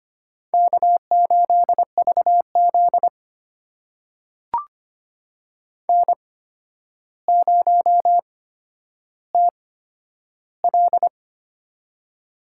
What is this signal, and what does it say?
Morse code, heard as a keyed tone, about 25 wpm, spaced as for 11 wpm, 700 Hz. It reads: K8VZ N0TL
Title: USA Call Signs - 4 Characters - 8x Character Spacing - 25wpm